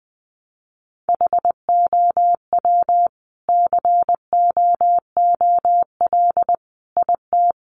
Morse code, heard as a keyed tone, about 20 wpm, 700 Hz. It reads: HOW COOL IT
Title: Sentences from Top 1200 Words 20wpm